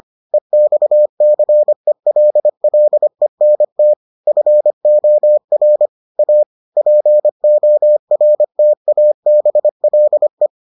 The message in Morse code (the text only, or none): EXCELLENT FOR A PORTABLE